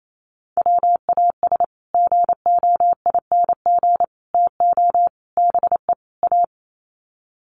Morse code, heard as a keyed tone, about 28 wpm, 700 Hz. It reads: WAS GOING TO BE A
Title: Sets of 5 Words 28wpm